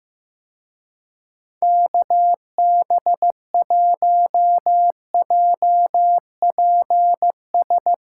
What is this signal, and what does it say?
Morse code, heard as a keyed tone, about 16 wpm, 700 Hz. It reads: KB1JPS
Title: Speed Racing - USA 2x3 Call Signs 15wpm